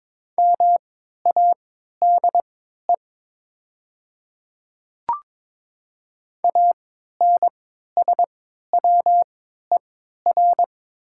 Morse code, heard as a keyed tone, about 22 wpm, spaced as for 12 wpm, 700 Hz. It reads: MADE ANSWER
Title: 3x Character Spacing - Top 300 Words - 22wpm